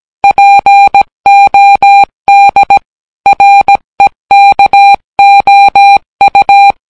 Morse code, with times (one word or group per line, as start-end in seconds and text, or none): 0.2-2.8 POD
3.3-6.7 REKOU